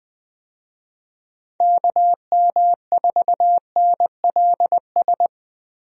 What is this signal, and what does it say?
Morse code, heard as a keyed tone, about 21 wpm, 700 Hz. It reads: KM4NLS
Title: USA Call Signs - 6 Characters - 20wpm